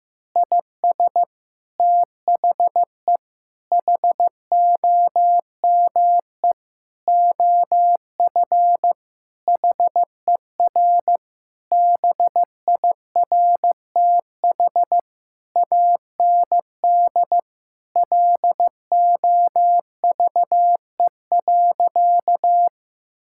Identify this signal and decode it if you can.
Morse code, heard as a keyed tone, about 15 wpm, 700 Hz. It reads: IS THE HOME OF HER BIRTH AND LOVE.
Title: Sentences from Top 1000 Words - Review Part 5 - 15wpm